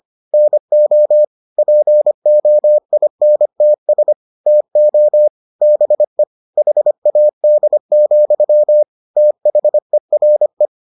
Morse code, heard as a keyed tone, about 25 wpm, 600 Hz. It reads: NO POINTS TO BE HAD, THERE